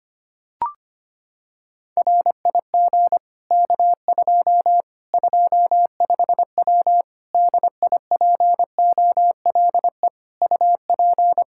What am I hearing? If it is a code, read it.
Morse code, heard as a keyed tone, about 25 wpm, 700 Hz. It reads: RIG K2 25W DIPOLE UP
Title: Practice QSO - Exchange 2 - Info - 25wpm